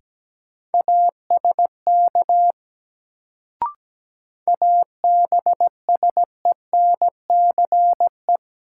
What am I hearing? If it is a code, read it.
Morse code, heard as a keyed tone, about 17 wpm, 700 Hz. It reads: ASK ABSENCE
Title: Sets of 4 Words - New Word Drill - 17wpm